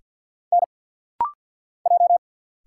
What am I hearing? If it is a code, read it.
Morse code, heard as a keyed tone, about 48 wpm, 690 Hz. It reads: N J
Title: Single Letter-Number - Rapid-Fire 50wpm